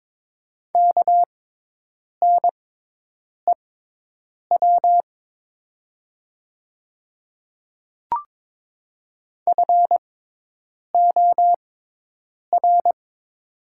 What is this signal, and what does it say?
Morse code, heard as a keyed tone, about 22 wpm, spaced as for 8 wpm, 700 Hz. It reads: KNEW FOR
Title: 6x Character Spacing - Top 500 Words - 22wpm